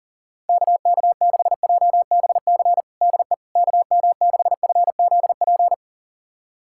Morse code, heard as a keyed tone, about 40 wpm, 700 Hz. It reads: KK6JBC DE KM6FZP
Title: USA Callsign DE Callsign - 6 Characters - 40wpm